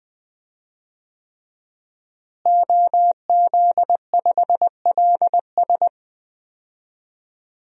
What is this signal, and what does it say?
Morse code, heard as a keyed tone, about 20 wpm, 700 Hz. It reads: OZ5LS
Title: DXCC-100 Call Signs - 5 Characters - Repeated 3x - 20wpm